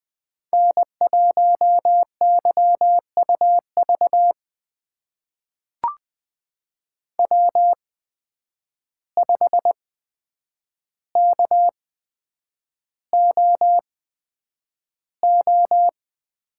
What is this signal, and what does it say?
Morse code, heard as a keyed tone, about 20 wpm, spaced as for 9 wpm, 700 Hz. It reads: N1YUV W5KOO
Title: USA Call Signs - 5 Characters - 8x Character Spacing - 20wpm